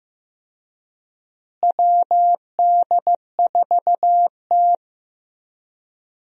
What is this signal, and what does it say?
Morse code, heard as a keyed tone, about 15 wpm, 700 Hz. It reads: WD4T
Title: USA Call Signs - 4 Characters - Repeated 3x - 15wpm